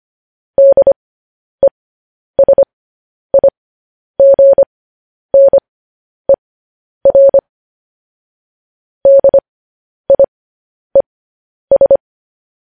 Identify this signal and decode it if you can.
Morse code, heard as a keyed tone, about 25 wpm, spaced as for 10 wpm, 550 Hz. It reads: DESIGNER DIES